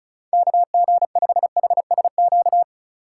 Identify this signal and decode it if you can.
Morse code, heard as a keyed tone, about 34 wpm, 700 Hz. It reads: KG5HSQ
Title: USA Call Signs - 6 Characters - 2x Character Spacing - 35wpm